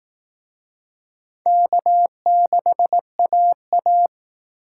Morse code, heard as a keyed tone, about 18 wpm, 700 Hz. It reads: K6AA
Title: Speed Racing - USA 1x2 Call Signs 15wpm